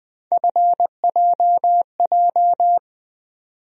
Morse code, heard as a keyed tone, about 20 wpm, 700 Hz. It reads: FJJ
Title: USA Call Sign Postfixes - 3 Letters 20wpm